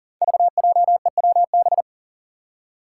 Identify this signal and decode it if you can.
Morse code, heard as a keyed tone, about 40 wpm, 700 Hz. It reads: VJEWB
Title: Five Challenging Letter-Number Combinations 40wpm